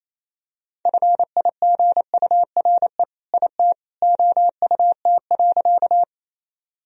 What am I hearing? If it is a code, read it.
Morse code, heard as a keyed tone, about 28 wpm, 700 Hz. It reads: FIGURE IT OUT.
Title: Sentences from Top 1000 Words - Review Part 2 - 28wpm